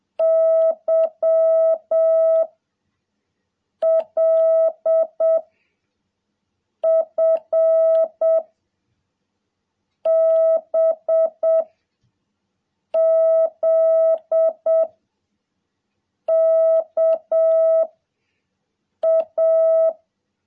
Morse code, heard as a keyed tone, about 7 wpm, 650 Hz. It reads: Y L F B Z K A
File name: jh8cbh / 2021cw